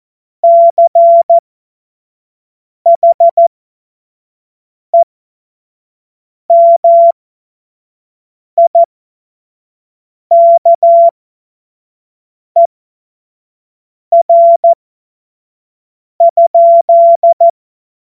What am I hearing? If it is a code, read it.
Morse code, heard as a keyed tone, about 14 wpm, spaced as for 5 wpm, 680 Hz. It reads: CHEMIKER?
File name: Morsekurs / Witze / Witz 06. Januar 2026 25 F70